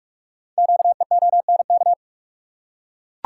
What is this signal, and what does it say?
Morse code, heard as a keyed tone, about 44 wpm, 690 Hz. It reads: QEONK
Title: Code Groups - 5 Letters - 45wpm